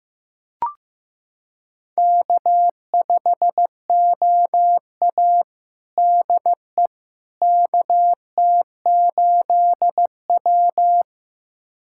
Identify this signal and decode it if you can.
Morse code, heard as a keyed tone, about 16 wpm, 700 Hz. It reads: K5OA DE KT8W
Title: USA Callsign DE Callsign - 4 Characters - 15wpm